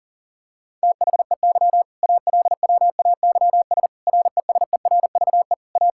Morse code, heard as a keyed tone, about 37 wpm, 700 Hz. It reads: THEY ALWAYS RESERVE A
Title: Sentences from Top 1000 Words - Review Part 6 - 40wpm